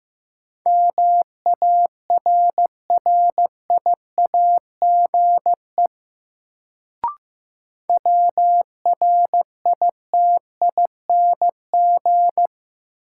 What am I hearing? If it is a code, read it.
Morse code, heard as a keyed tone, about 15 wpm, 700 Hz. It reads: MARRIAGE WRITING
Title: Sentences from Top 800 Words - New Word Drill - 15wpm